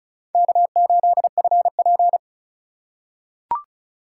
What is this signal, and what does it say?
Morse code, heard as a keyed tone, about 34 wpm, 700 Hz. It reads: K8FP
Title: USA Call Signs - 4 Characters - 2x Character Spacing - 35wpm